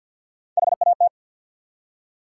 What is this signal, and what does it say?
Morse code, heard as a keyed tone, about 53 wpm, 700 Hz. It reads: HAT
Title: Three Letter Words - 50wpm